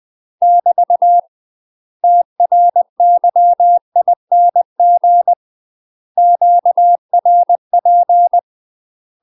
Morse code, heard as a keyed tone, about 20 wpm, 700 Hz. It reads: <BT> TRYING QRP
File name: UK MASTER QSOs TRAINING / 20WPM / UK MASTER QSOs 0073